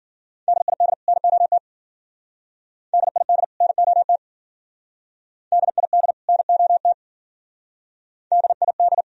Morse code, heard as a keyed tone, about 55 wpm, 700 Hz. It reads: DID NOT DID NOT DID NOT DID